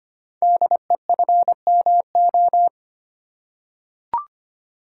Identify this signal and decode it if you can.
Morse code, heard as a keyed tone, about 25 wpm, 690 Hz. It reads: DEFMO
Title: Code Groups - 5 Letters - 25wpm